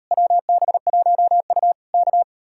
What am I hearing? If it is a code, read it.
Morse code, heard as a keyed tone, about 38 wpm, 700 Hz. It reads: WB1U K